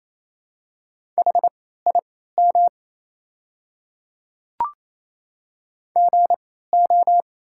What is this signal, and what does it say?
Morse code, heard as a keyed tone, about 28 wpm, spaced as for 16 wpm, 700 Hz. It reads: HIM GO
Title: 3x Character Spacing - Top 500 Words - 28wpm